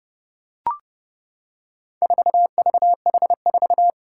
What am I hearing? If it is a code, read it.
Morse code, heard as a keyed tone, about 30 wpm, 700 Hz. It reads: 4VH4
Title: Four Challenging Letter-Number Combinations 30wpm